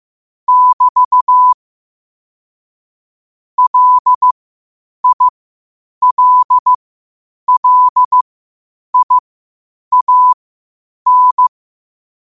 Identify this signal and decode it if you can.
Morse code, heard as a keyed tone, about 15 wpm, spaced as for 8 wpm, 1000 Hz. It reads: <BT> LILLIAN
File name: qotdfiles / 8wpm